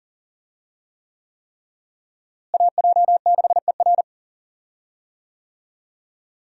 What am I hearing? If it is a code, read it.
Morse code, heard as a keyed tone, about 40 wpm, 700 Hz. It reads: AJ6ER
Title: USA Call Signs - 5 Characters - Repeated 3x - 40wpm